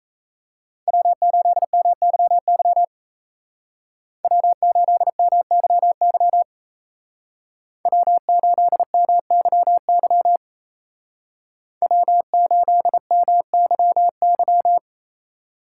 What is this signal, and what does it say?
Morse code, heard as a keyed tone, about 36 wpm, 700 Hz. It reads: W8MYY W8MYY W8MYY W8MYY